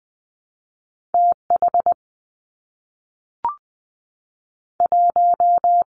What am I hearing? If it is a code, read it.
Morse code, heard as a keyed tone, about 20 wpm, 700 Hz. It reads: T H 1
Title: Two Challenging Letter-Number Combinations 20wpm